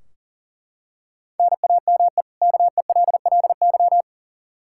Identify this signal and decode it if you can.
Morse code, heard as a keyed tone, about 40 wpm, 700 Hz. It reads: NAME KELLY
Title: Practice QSO - Exchange 1 - Beginning - 40wpm